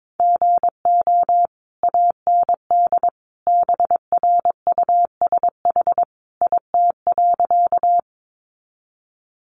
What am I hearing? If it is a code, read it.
Morse code, heard as a keyed tone, about 22 wpm, 700 Hz. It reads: GO AND BRUSH IT.
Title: Sentences from Top 1100 Words 22wpm